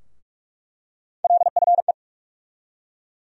Morse code, heard as a keyed tone, about 44 wpm, 700 Hz. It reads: LFE
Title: USA Call Sign Postfixes - 3 Letters 45wpm